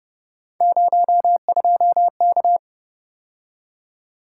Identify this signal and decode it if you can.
Morse code, heard as a keyed tone, about 31 wpm, 700 Hz. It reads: 02K
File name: Speed-Racing - Three Letter-Number Combinations - 25wpm